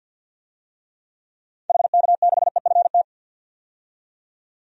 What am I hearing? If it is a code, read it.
Morse code, heard as a keyed tone, about 52 wpm, 700 Hz. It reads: HK6EFT
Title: DXCC-100 Call Signs - 6 Characters - 50wpm